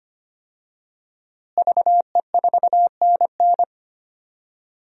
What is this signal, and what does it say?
Morse code, heard as a keyed tone, about 25 wpm, 700 Hz. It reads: VE4NN
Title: DXCC-100 Call Signs - 5 Characters - Repeated 2x - 25wpm